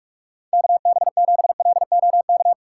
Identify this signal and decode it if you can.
Morse code, heard as a keyed tone, about 41 wpm, 700 Hz. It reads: KB7LOK
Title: USA Call Signs - 6 Characters - 4x Character Spacing - 45wpm